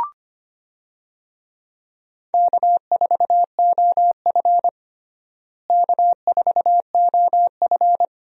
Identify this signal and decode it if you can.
Morse code, heard as a keyed tone, about 25 wpm, 700 Hz. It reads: K4OF K4OF
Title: USA Call Signs - 4 Characters - Repeated 2x - 25wpm